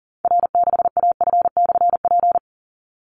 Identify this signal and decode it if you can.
Morse code, heard as a keyed tone, about 40 wpm, 700 Hz. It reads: R6AF/P
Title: Call Signs Repeated 2x - 40wpm